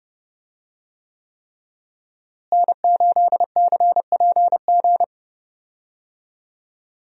Morse code, heard as a keyed tone, about 30 wpm, 700 Hz. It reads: N8CPG
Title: USA Call Signs - 5 Characters - Repeated 3x - 30wpm